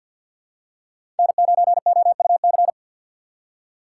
Morse code, heard as a keyed tone, about 53 wpm, 700 Hz. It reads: N9OUC
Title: USA Call Signs - 5 Characters - 2x Character Spacing - 50wpm